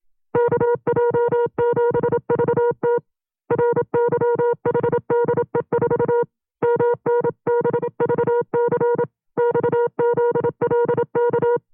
Morse code, heard as a keyed tone, about 27 wpm, 480 Hz. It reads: KJ7VT RYHDE4 MNBVC XZLK
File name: FreeFiles / CW-practice / 25WPM-side2